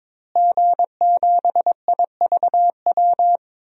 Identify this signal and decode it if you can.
Morse code, heard as a keyed tone, about 22 wpm, 700 Hz. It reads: G7IVW
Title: DXCC-100 Call Signs - 5 Characters - Repeated 3x - 22wpm